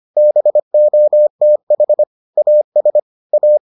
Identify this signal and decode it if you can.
Morse code, heard as a keyed tone, about 25 wpm, 600 Hz. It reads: BOTH AS A